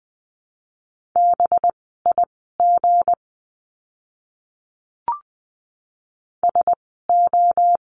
Morse code, heard as a keyed tone, about 20 wpm, spaced as for 14 wpm, 700 Hz. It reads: BIG SO
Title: 244 Lesson 39f - Letter z - Words 20wpm